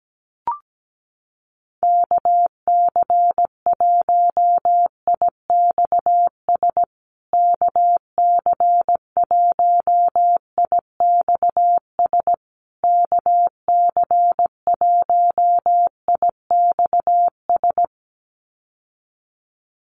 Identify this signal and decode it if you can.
Morse code, heard as a keyed tone, about 17 wpm, 700 Hz. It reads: KC1IXS KC1IXS KC1IXS
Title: USA Call Signs Repeated 3x 17wpm